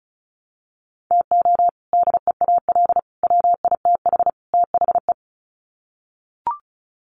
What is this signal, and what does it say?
Morse code, heard as a keyed tone, about 34 wpm, 700 Hz. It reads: TO DEAL WITH THE
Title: Sets of 4 Words 35wpm